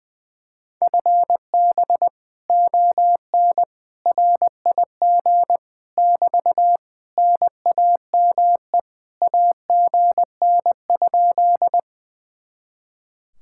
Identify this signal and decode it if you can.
Morse code, heard as a keyed tone, about 20 wpm, 700 Hz. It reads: FB ON RIG <BT> NAME AGN?
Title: Speed-Racing Common QSO Phrases - 20wpm